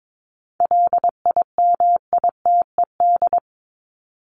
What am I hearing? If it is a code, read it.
Morse code, heard as a keyed tone, about 22 wpm, 700 Hz. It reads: LIMITED
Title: Top 751-1000 Words 22wpm